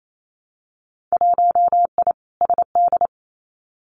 Morse code, heard as a keyed tone, about 28 wpm, 700 Hz. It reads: 1I SD
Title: Winter Field Day Exchanges 28wpm